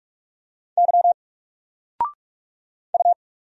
Morse code, heard as a keyed tone, about 44 wpm, 700 Hz. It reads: Y U
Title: Single Letter-Number Alt 45wpm